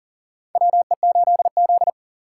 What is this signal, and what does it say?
Morse code, heard as a keyed tone, about 40 wpm, 700 Hz. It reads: WE8Z
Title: USA Call Signs - 6x Character Spacing - 40wpm